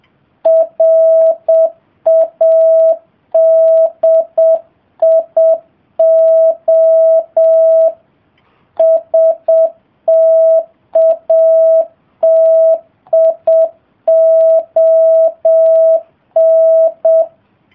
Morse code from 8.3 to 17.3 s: STATION